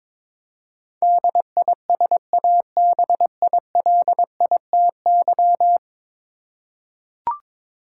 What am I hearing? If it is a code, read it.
Morse code, heard as a keyed tone, about 22 wpm, 700 Hz. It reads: DISABILITY
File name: practice / 22wpm / 10 Letter Words - First 3 Letters Given - 22wpm